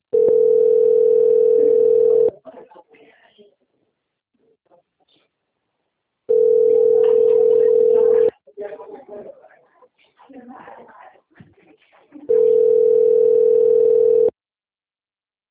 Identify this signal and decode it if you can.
Ringback tone (North America), 440 + 480 Hz, with speech.
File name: out- - - - - .13993